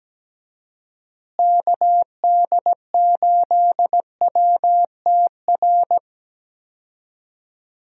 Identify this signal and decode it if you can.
Morse code, heard as a keyed tone, about 17 wpm, 700 Hz. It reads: KD8WTR
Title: USA Call Signs - 6 Characters - Repeated 3x - 17wpm